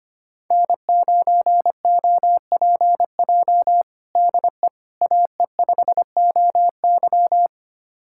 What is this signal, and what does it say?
Morse code, heard as a keyed tone, about 25 wpm, 700 Hz. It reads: N9OPJ DE AE5OY
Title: USA Callsign DE Callsign - 5 Characters - 25wpm